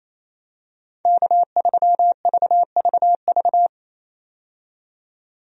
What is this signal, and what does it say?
Morse code, heard as a keyed tone, about 28 wpm, 700 Hz. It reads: K3VVV
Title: USA Call Signs - 5 Characters - 28wpm